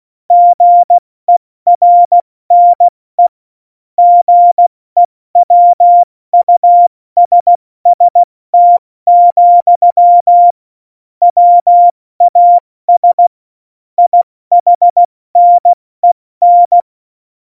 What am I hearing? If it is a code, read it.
Morse code, heard as a keyed tone, about 16 wpm, 700 Hz. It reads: GERNE GEWUSST, WAS IHNEN